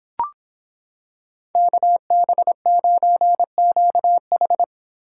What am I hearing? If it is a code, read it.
Morse code, heard as a keyed tone, about 26 wpm, 700 Hz. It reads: KB9QH